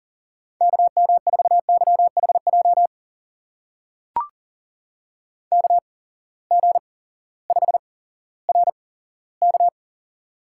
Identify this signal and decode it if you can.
Morse code, heard as a keyed tone, about 40 wpm, spaced as for 18 wpm, 700 Hz. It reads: KM4YHJ KG5RK
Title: USA Call Signs - 8x Character Spacing - 40wpm